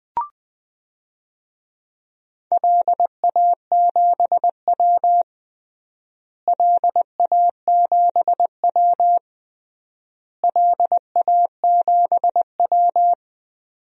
Morse code, heard as a keyed tone, about 21 wpm, 700 Hz. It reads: LA7W LA7W LA7W
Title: DXCC-100 Call Signs - 4 Characters - Repeated 3x - 20wpm